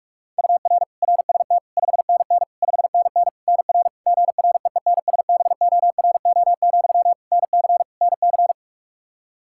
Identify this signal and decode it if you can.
Morse code, heard as a keyed tone, about 46 wpm, 700 Hz. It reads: UR RST 5NN 5NN NR GREENSBORO, NC NC
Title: Practice QSO - Exchange 1 - Beginning - 45wpm